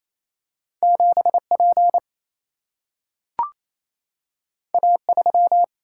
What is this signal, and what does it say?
Morse code, heard as a keyed tone, about 28 wpm, 700 Hz. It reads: 7P A3
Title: Single-Letter Single-Number 28wpm